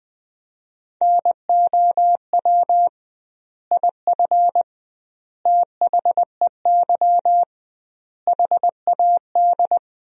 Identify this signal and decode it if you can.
Morse code, heard as a keyed tone, about 21 wpm, 700 Hz. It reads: NOW IF THEY HAD
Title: 2x Word-Spacing - Sentences from Top 500 Words - 20wpm